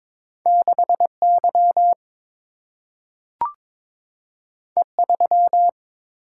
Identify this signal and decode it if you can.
Morse code, heard as a keyed tone, about 22 wpm, 700 Hz. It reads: 6Y E3
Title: Single-Letter Single-Number 22wpm